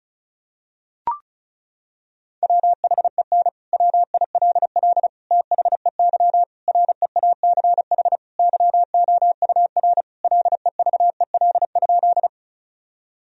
Morse code, heard as a keyed tone, about 36 wpm, 700 Hz. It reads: WHEN WILL THEY REACH YOUR LEVEL?
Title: Sentences from Top 1000 Words - Review Part 4 - 35wpm